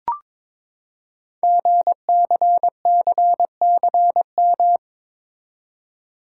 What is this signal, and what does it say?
Morse code, heard as a keyed tone, about 22 wpm, 700 Hz. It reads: GCCCM
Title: Code Groups - 5 Letters - 22wpm